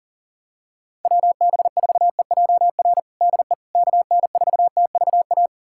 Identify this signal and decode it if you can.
Morse code, heard as a keyed tone, about 37 wpm, 700 Hz. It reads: WB4EJR DE KN4TVA
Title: USA Callsign DE Callsign 40wpm